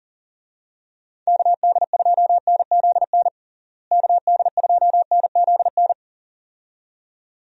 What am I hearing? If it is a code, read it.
Morse code, heard as a keyed tone, about 39 wpm, 700 Hz. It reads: KD2NZN KD2NZN
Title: USA Call Signs - 6 Characters - Repeated 2x - 40wpm